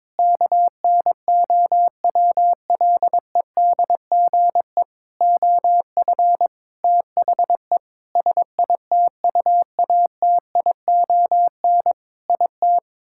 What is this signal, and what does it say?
Morse code, heard as a keyed tone, about 22 wpm, 700 Hz. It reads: KNOWLEDGE OF THE SITUATION IT
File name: Sentences with 9-Word Phrases - 22wpm